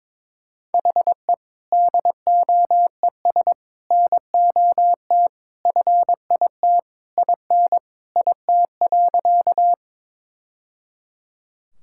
Morse code, heard as a keyed tone, about 22 wpm, 700 Hz. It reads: HE DOES NOT FIT IN IT.
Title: Sentences from Top 1000 Words - Review Part 5 - 22wpm